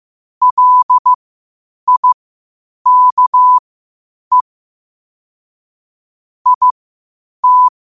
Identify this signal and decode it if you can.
Morse code, heard as a keyed tone, about 15 wpm, spaced as for 8 wpm, 1000 Hz. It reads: LIKE IT
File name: qotdfiles / 8wpm